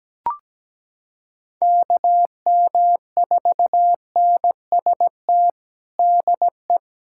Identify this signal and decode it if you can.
Morse code, heard as a keyed tone, about 17 wpm, 700 Hz. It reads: KM4NST DE
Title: USA Callsign DE Callsign 17wpm